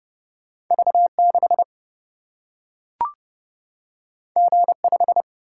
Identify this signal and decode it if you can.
Morse code, heard as a keyed tone, about 29 wpm, 700 Hz. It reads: V 6 G 5
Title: Single-Letter Single-Number 30wpm